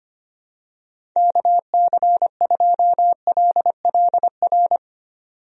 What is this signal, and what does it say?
Morse code, heard as a keyed tone, about 25 wpm, 700 Hz. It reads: KC2LLR
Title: USA Call Signs - 6 Characters - 25wpm